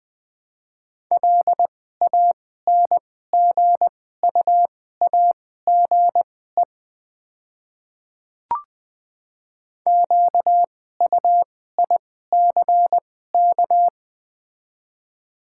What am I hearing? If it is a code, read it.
Morse code, heard as a keyed tone, about 20 wpm, spaced as for 14 wpm, 700 Hz. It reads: LANGUAGE QUICK
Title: 2x Character Spacing - Top 500 Words - 20wpm